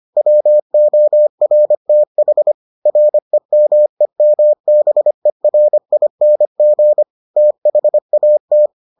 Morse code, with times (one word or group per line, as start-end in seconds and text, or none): 0.2-2.5 WORTH
2.9-7.0 REMEMBERING
7.4-8.7 THAT